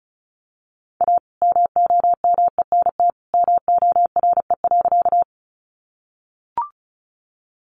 Morse code, heard as a keyed tone, about 34 wpm, 700 Hz. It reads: A MOMENT MORE.
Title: Sentences from Top 1000 Words 35wpm